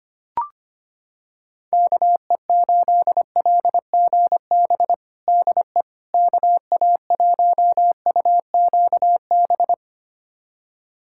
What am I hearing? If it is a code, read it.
Morse code, heard as a keyed tone, about 25 wpm, 700 Hz. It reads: KE8LGB DE KA1UQB
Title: USA Callsign DE Callsign - 6 Characters - 25wpm